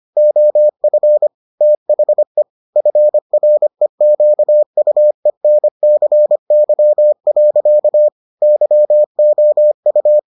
Morse code, heard as a keyed tone, about 25 wpm, 600 Hz. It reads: OF THE FREQUENCY. YOU